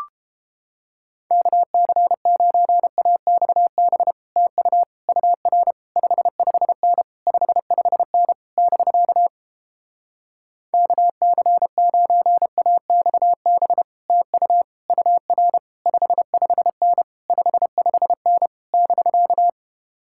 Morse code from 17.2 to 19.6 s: 55N <BK>

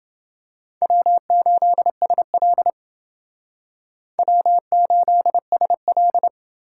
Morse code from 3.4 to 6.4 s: W8SL

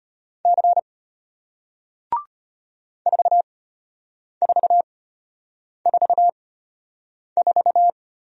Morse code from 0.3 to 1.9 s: C

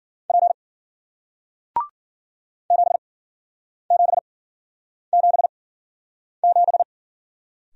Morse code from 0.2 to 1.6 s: P

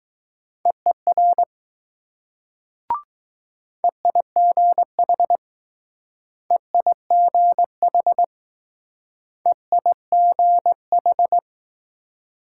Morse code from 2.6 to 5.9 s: EIGH

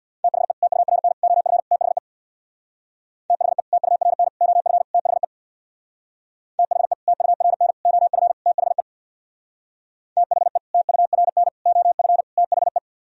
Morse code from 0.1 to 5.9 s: THE TURN OF THE THE TURN OF THE

